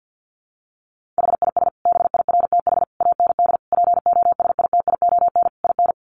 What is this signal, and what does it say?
Morse code, heard as a keyed tone, about 50 wpm, 700 Hz. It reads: HIS BIRTH AND POSITION IN
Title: Sentences from Top 900 Words 50wpm